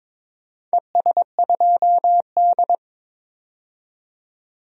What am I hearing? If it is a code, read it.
Morse code, heard as a keyed tone, about 22 wpm, 700 Hz. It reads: ES2D
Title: DXCC-100 Call Signs - 4 Characters - 22wpm